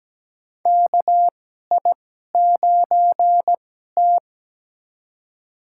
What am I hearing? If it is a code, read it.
Morse code, heard as a keyed tone, about 17 wpm, spaced as for 12 wpm, 700 Hz. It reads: KI9T
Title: USA Call Signs - 4 Characters - 2x Character Spacing - 17wpm